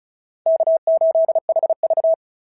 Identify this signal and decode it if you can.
Morse code, heard as a keyed tone, about 36 wpm, 690 Hz. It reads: K8HV
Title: SOTA QSO Practice 35wpm